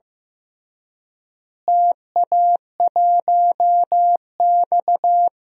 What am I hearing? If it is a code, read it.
Morse code, heard as a keyed tone, about 15 wpm, 700 Hz. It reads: TA1X
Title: DXCC-100 Call Signs - 4 Characters - Repeated 3x - 15wpm